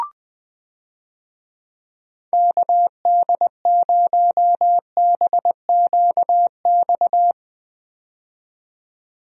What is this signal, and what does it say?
Morse code, heard as a keyed tone, about 20 wpm, 700 Hz. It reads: KD0BQX
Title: USA Call Signs - 6 Characters - 20wpm